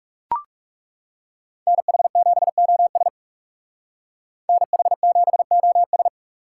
Morse code, heard as a keyed tone, about 38 wpm, 700 Hz. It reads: NH7OS NH7OS